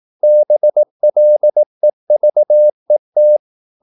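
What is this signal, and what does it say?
Morse code, heard as a keyed tone, about 18 wpm, 600 Hz. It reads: BLEVET
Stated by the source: Slow (QRS) Morse practice at 17 wpm, from CW med Gnister.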